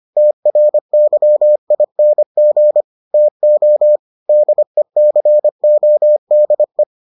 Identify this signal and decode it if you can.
Morse code, heard as a keyed tone, about 25 wpm, 600 Hz. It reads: TRYING TO DECODE